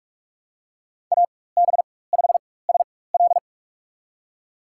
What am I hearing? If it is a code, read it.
Morse code, heard as a keyed tone, about 47 wpm, spaced as for 21 wpm, 700 Hz. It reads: AB5SL
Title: USA Call Signs - 5 Characters - 4x Character Spacing - 45wpm